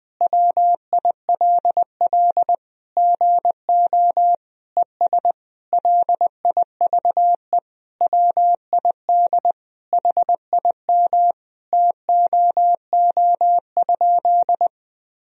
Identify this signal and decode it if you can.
Morse code, heard as a keyed tone, about 21 wpm, 700 Hz. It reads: WILL GO ES LIVE WID HIM TOO?
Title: Sentences from Top 500 Words and Abbreviations 20wpm